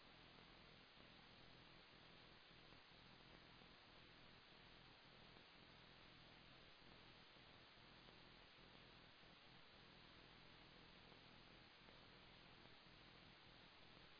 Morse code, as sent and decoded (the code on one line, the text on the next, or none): none